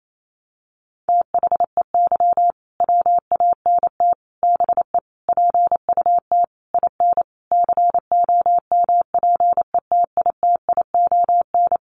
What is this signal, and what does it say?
Morse code, heard as a keyed tone, about 28 wpm, 700 Hz. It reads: THEY WANT BE PUT IN COMPETITION
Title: Sentences from Top 1000 Words 28wpm